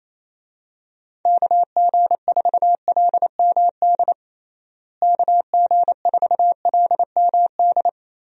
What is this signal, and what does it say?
Morse code, heard as a keyed tone, about 28 wpm, 700 Hz. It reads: KG4LMD KG4LMD
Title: USA Call Signs - 6 Characters - Repeated 2x - 28wpm